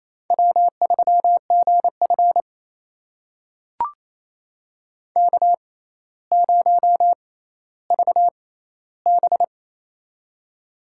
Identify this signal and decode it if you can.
Morse code, heard as a keyed tone, about 28 wpm, spaced as for 14 wpm, 700 Hz. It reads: W3GF K0VB
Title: USA Call Signs - 4 Characters - 6x Character Spacing - 28wpm